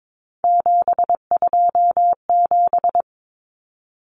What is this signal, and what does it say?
Morse code, heard as a keyed tone, about 22 wpm, 700 Hz. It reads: 727
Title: Numbers - 3-Digit 22wpm